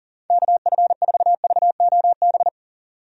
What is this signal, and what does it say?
Morse code, heard as a keyed tone, about 37 wpm, 700 Hz. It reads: KF4VOB
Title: USA Call Signs - 6 Characters - 40wpm